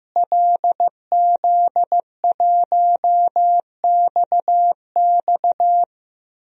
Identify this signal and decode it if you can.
Morse code, heard as a keyed tone, about 15 wpm, 700 Hz. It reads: LZ1XX
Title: DXCC-100 Call Signs - 5 Characters - Repeated 2x - 15wpm